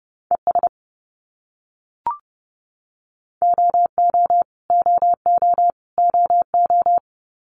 Morse code, heard as a keyed tone, about 30 wpm, 700 Hz. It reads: ES OO OO OO
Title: Abbreviations Repeated 3x - 30wpm